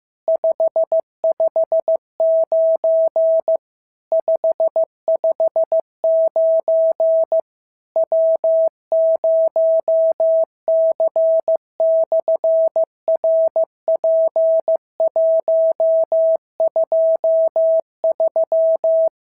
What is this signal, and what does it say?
Morse code, heard as a keyed tone, about 15 wpm, 650 Hz. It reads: 559 559 W0C/RP123
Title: SOTA QSO Practice 15wpm